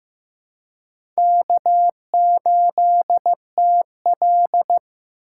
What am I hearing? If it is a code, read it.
Morse code, heard as a keyed tone, about 15 wpm, 700 Hz. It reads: K8TL
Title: USA Call Signs - 4 Characters - 2x Character Spacing - 15wpm